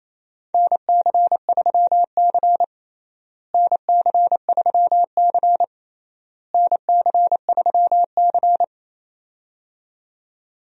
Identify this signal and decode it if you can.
Morse code, heard as a keyed tone, about 28 wpm, 700 Hz. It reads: NC3C NC3C NC3C
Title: USA Call Signs - 4 Characters - Repeated 3x - 28wpm